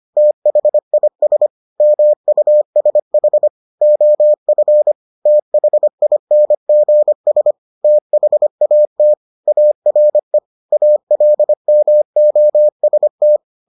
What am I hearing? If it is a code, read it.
Morse code, heard as a keyed tone, about 25 wpm, 600 Hz. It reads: THIS MUSH OF THINGS THAT ARE ALMOST